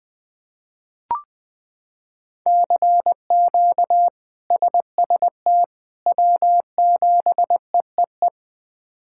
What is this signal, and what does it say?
Morse code, heard as a keyed tone, about 20 wpm, 700 Hz. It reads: CQ SST W7EEE
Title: K1USN SST QSO Practice - 20wpm